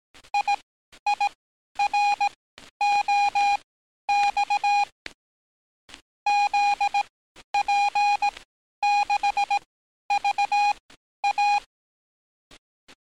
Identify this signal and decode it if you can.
Morse code, heard as a keyed tone, about 18 wpm, spaced as for 11 wpm, 820 Hz. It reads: IIROX ZP6VA
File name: training / NAVPERS 11049to11060 / NAVPERS 11050 RA RB